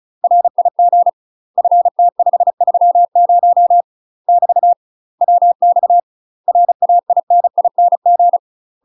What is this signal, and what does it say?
Morse code, heard as a keyed tone, about 34 wpm, 700 Hz. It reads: RIG FT530 <BT> WX RAINING
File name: UK MASTER QSOs TRAINING / 35WPM / UK MASTER QSOs 0008